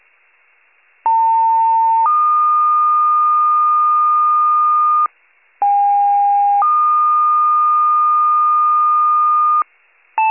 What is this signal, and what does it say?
Two-tone page, heard as a one-second tone, then a three-second tone, about 900 then 1220 Hz, about 810 then 1220 Hz. It is heard from 1.1 to 9.6 s.